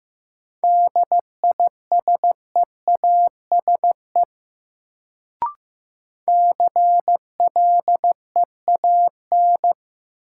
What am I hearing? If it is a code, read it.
Morse code, heard as a keyed tone, about 15 wpm, 700 Hz. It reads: DISEASE CLEAN
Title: Sentences from Top 1100 Words - New Word Drill Alt - 15wpm